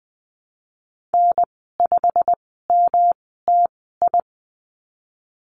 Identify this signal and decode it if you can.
Morse code, heard as a keyed tone, about 20 wpm, spaced as for 15 wpm, 700 Hz. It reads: N5MTI
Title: 140 Lesson 23g - Letter f - Call Signs 20wpm